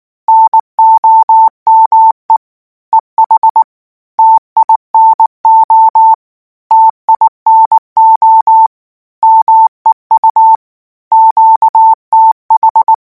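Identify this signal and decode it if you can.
Morse code, heard as a keyed tone, about 19 wpm, 890 Hz. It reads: NOME EH TINO TINO MEU QTH
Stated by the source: Morse code practice text at 18 wpm, from Portal do CW.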